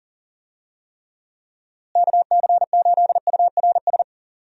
Morse code, heard as a keyed tone, about 37 wpm, 700 Hz. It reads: KC8URS
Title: USA Call Signs - 6 Characters - Repeated 3x - 40wpm